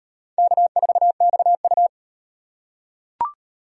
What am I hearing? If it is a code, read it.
Morse code, heard as a keyed tone, about 36 wpm, 700 Hz. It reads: K4XU